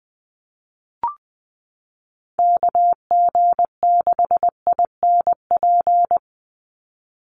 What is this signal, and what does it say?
Morse code, heard as a keyed tone, about 20 wpm, 700 Hz. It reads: KG6INP
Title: USA Call Signs 20wpm